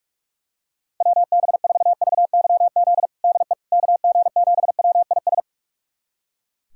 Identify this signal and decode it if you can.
Morse code, heard as a keyed tone, about 45 wpm, 700 Hz. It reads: WB4VYZ DE KG7WIS
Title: USA Callsign DE Callsign 45wpm